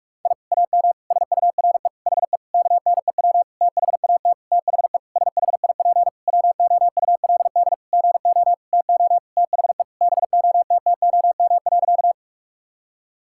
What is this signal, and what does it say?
Morse code, heard as a keyed tone, about 41 wpm, 700 Hz. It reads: I AM SURE HE KNEW THAT THE SHIP WOULD GO TO THE BOTTOM.
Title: Sentences from Top 1000 Words - Review Part 6 - 45wpm